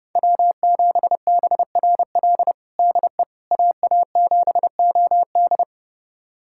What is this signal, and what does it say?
Morse code, heard as a keyed tone, about 30 wpm, 700 Hz. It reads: W7BRL DE AA7OD
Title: USA Callsign DE Callsign - 5 Characters - 30wpm